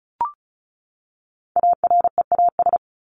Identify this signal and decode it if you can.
Morse code, heard as a keyed tone, about 34 wpm, 700 Hz. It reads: AREAS
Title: ES:S Suffix - Top 100 Words with Base Words 35wpm